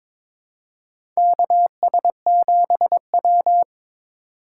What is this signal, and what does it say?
Morse code, heard as a keyed tone, about 21 wpm, 700 Hz. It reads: KS7W
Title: USA Call Signs - 4 Characters - 6x Character Spacing - 22wpm